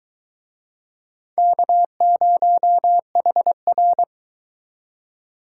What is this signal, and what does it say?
Morse code, heard as a keyed tone, about 22 wpm, 700 Hz. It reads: K0HR